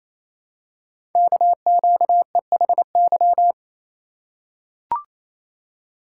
Morse code, heard as a keyed tone, about 27 wpm, 700 Hz. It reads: KQEHY
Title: Code Groups - 5 Letters - 28wpm